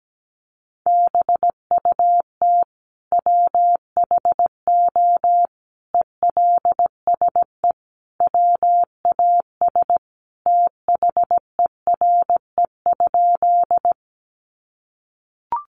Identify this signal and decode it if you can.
Morse code, heard as a keyed tone, about 17 wpm, 700 Hz. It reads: BUT WHO ELSE WAS THERE?
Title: Sentences from Top 800 Words 17wpm